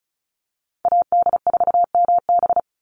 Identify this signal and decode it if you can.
Morse code, heard as a keyed tone, about 34 wpm, 700 Hz. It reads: AD4MB
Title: USA Call Signs 35wpm